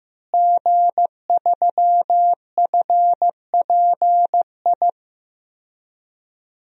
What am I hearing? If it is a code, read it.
Morse code, heard as a keyed tone, about 16 wpm, 700 Hz. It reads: G3FPI
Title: DXCC-100 Call Signs - 5 Characters - Repeated 2x - 15wpm